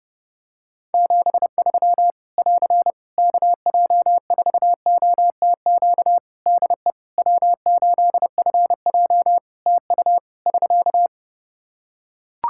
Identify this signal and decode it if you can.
Morse code, heard as a keyed tone, about 30 wpm, 700 Hz. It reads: 73 <AR> KJ4OTQ DE W8FJ TU <SK>
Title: Practice QSO - Exchange 4 - Ending - 30wpm